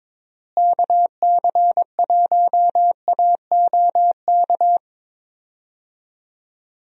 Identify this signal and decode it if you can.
Morse code, heard as a keyed tone, about 22 wpm, 700 Hz. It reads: KC1AOK
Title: USA Call Signs - 6 Characters - Repeated 2x - 22wpm